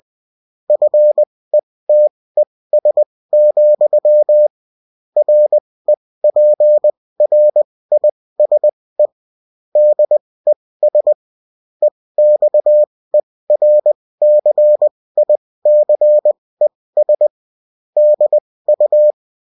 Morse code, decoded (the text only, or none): FETES, REPRISE DES EXERCICES DU